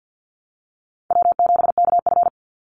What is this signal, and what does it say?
Morse code, heard as a keyed tone, about 50 wpm, 700 Hz. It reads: W7KF
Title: Call Signs - 50wpm